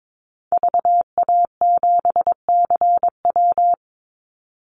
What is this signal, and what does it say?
Morse code, heard as a keyed tone, about 22 wpm, 700 Hz. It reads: VA7CW
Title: Call Signs - 22wpm